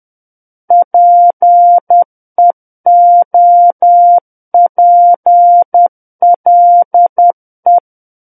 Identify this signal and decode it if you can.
Morse code, heard as a keyed tone, about 10 wpm, 700 Hz. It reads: PEOPLE
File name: Book One / 01-Chapter 1 - 10 WPM